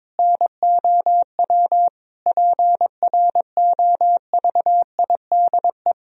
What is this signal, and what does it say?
Morse code, heard as a keyed tone, about 22 wpm, 700 Hz. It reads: NOW PROVIDE
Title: Sentences from Top 1000 Words - Review Part 3 - 22wpm